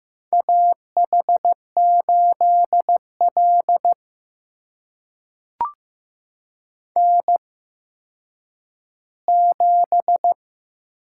Morse code, heard as a keyed tone, about 15 wpm, spaced as for 9 wpm, 700 Hz. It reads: AH8L N7
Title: USA Call Signs - 4 Characters - 8x Character Spacing - 15wpm